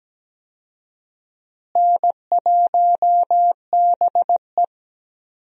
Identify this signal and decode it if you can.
Morse code, heard as a keyed tone, about 17 wpm, 700 Hz. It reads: N1BE
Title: USA Call Signs - 4 Characters - Repeated 3x - 17wpm